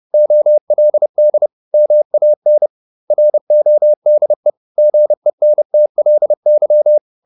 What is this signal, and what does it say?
Morse code, heard as a keyed tone, about 30 wpm, 600 Hz. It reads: OLD MAN RODE GENTLY
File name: the old man and the sea / the old man and the sea 1 30wpm